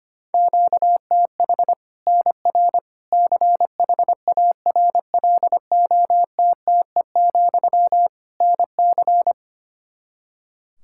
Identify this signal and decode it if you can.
Morse code, heard as a keyed tone, about 25 wpm, 700 Hz. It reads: QTH NR CHARLOTTE, NC
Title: Practice QSO - Exchange 1 - Beginning - 25wpm